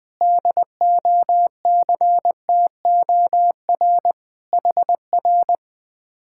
Morse code, heard as a keyed tone, about 21 wpm, 700 Hz. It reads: DOCTOR HR
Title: Speed-Racing Common QSO Phrases - 20wpm